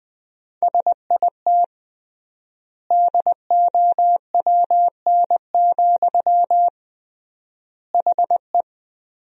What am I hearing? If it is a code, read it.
Morse code, heard as a keyed tone, about 21 wpm, 700 Hz. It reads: SIT DOWN, HE
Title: 3x Word-Spacing - Sentences from Top 500 Words - 20wpm